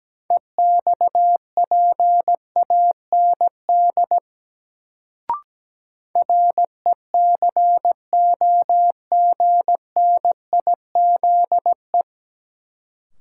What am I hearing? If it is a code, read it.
Morse code, heard as a keyed tone, about 17 wpm, 700 Hz. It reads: EXPAND RECOGNIZE
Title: Sentences from Top 1400 Words - New Word Drill Alt - 17wpm